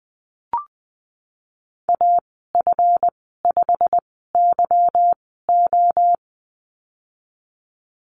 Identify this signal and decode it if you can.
Morse code, heard as a keyed tone, about 20 wpm, spaced as for 15 wpm, 700 Hz. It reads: AF5YO
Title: 173 Lesson 28g - Number 9 - Call Signs 20wpm